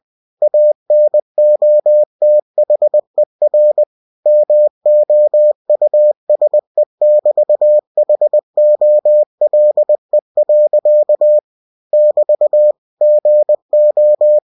Morse code, heard as a keyed tone, about 20 wpm, 600 Hz. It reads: ANOTHER MOUSE<BT>HOLE. <BT> GO